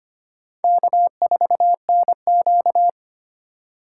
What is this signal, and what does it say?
Morse code, heard as a keyed tone, about 31 wpm, 700 Hz. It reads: K4NQ